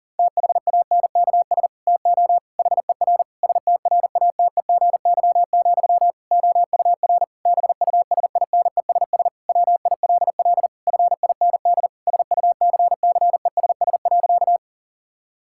Morse code, heard as a keyed tone, about 40 wpm, 700 Hz. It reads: THANKS TO HER STRATEGY, OUR BUSINESS WILL FIND SUCCESS.
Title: Sentences from Top 1000 Words - Review Part 5 - 40wpm